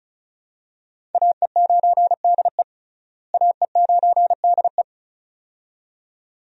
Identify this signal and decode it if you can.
Morse code, heard as a keyed tone, about 36 wpm, 700 Hz. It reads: AE9DE AE9DE
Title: USA Call Signs - 5 Characters - Repeated 2x - 35wpm